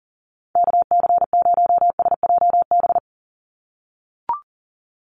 Morse code, heard as a keyed tone, about 40 wpm, 700 Hz. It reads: KC0SJB
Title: USA Call Signs 40wpm